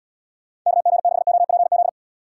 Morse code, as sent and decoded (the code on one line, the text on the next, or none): -.- --. -.... --- .--. --..
KG6OPZ